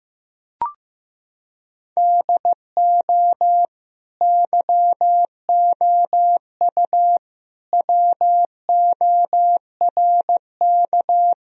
Morse code, heard as a keyed tone, about 15 wpm, 700 Hz. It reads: DO YOU WORK
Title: Sentences from Top 1000 Words - Review Part 4 - 15wpm